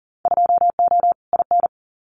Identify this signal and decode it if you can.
Morse code, heard as a keyed tone, about 40 wpm, 700 Hz. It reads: 2O IN
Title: Winter Field Day Exchanges 40wpm